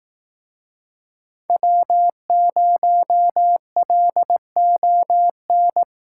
Morse code, heard as a keyed tone, about 18 wpm, 700 Hz. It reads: W0LON
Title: Speed Racing - USA 1x3 Call Signs 15wpm